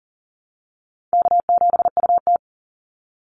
Morse code, heard as a keyed tone, about 40 wpm, 700 Hz. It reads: K7UT
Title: Easy Call Signs Repeated 3x - 40wpm